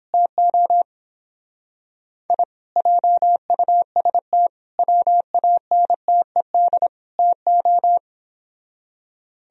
Morse code, heard as a keyed tone, about 29 wpm, 700 Hz. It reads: TO I JUST WANTED TO